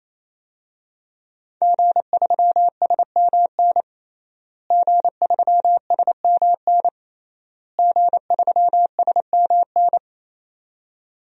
Morse code, heard as a keyed tone, about 28 wpm, 700 Hz. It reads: G3SMN G3SMN G3SMN
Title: DXCC-100 Call Signs - 5 Characters - Repeated 3x - 28wpm